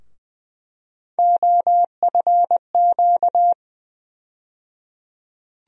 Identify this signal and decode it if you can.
Morse code, heard as a keyed tone, about 20 wpm, 700 Hz. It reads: OFQ
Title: USA Call Sign Postfixes - 3 Letters 20wpm